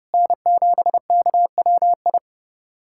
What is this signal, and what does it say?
Morse code, heard as a keyed tone, about 31 wpm, 700 Hz. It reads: N7KWI
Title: USA Callsign DE Callsign 30wpm